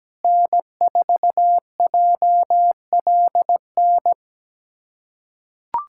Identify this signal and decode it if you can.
Morse code, heard as a keyed tone, about 17 wpm, 700 Hz. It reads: N4JLN
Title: USA Callsign DE Callsign - 5 Characters - 17wpm